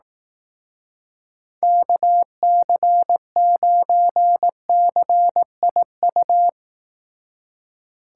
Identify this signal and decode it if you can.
Morse code, heard as a keyed tone, about 18 wpm, 700 Hz. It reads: KC9CIU
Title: Speed Racing - USA 2x3 Call Signs 15wpm